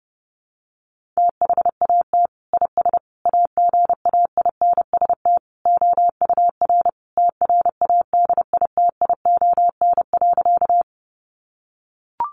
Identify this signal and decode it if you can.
Morse code, heard as a keyed tone, about 30 wpm, 700 Hz. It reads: THAT IS AGAINST OUR TRADITION.
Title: Sentences from Top 1500 Words 30wpm